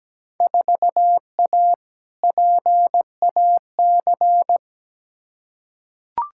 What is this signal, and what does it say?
Morse code, heard as a keyed tone, about 17 wpm, 700 Hz. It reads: 4A PAC
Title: Field Day Exchanges Encore - 17wpm